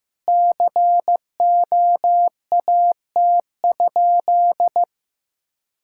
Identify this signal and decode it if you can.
Morse code, heard as a keyed tone, about 15 wpm, 700 Hz. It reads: COAT?
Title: Sentences from Top 1600 Words 15wpm